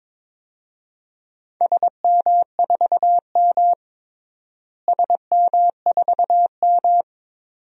Morse code, heard as a keyed tone, about 21 wpm, 700 Hz. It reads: SM4M SM4M
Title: DXCC-100 Call Signs - 4 Characters - Repeated 2x - 22wpm